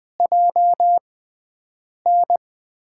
Morse code, heard as a keyed tone, about 20 wpm, spaced as for 7 wpm, 700 Hz. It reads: JN
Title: USA Call Signs - 5 Characters - 6x Character Spacing - 20wpm